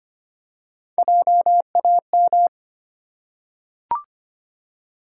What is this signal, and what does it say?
Morse code, heard as a keyed tone, about 25 wpm, 690 Hz. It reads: JAM
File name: practice / 25wpm / Easy Words with Hints - 25wpm